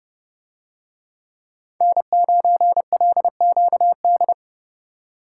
Morse code, heard as a keyed tone, about 30 wpm, 700 Hz. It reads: N9LQD